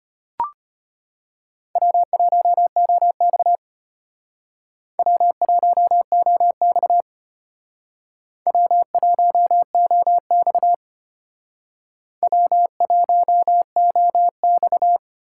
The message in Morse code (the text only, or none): W1OX W1OX W1OX W1OX